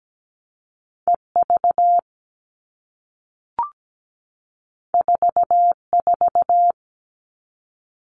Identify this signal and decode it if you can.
Morse code, heard as a keyed tone, about 17 wpm, 700 Hz. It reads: EV 44
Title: All Cut Numbers - 2-Digit 17wpm